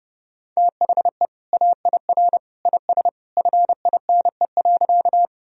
Morse code, heard as a keyed tone, about 30 wpm, 700 Hz. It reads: THE AIR IS FINE.
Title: Sentences from Top 1000 Words - Review Part 1 - 30wpm